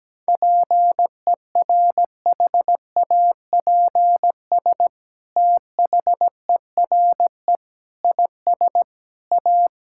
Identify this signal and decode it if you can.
Morse code, heard as a keyed tone, about 17 wpm, 700 Hz. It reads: PERHAPS THERE IS A
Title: Sentences from Top 1000 Words - Review Part 5 - 17wpm